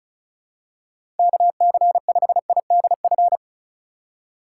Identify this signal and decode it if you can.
Morse code, heard as a keyed tone, about 34 wpm, 700 Hz. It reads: KC5IDF
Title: USA Call Signs - 6 Characters - 3x Character Spacing - 35wpm